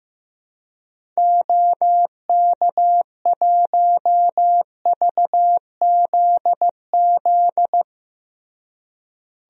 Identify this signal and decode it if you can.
Morse code, heard as a keyed tone, about 15 wpm, 700 Hz. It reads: OK1VZZ
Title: DXCC-100 Call Signs - 6 Characters - Repeated 2x - 15wpm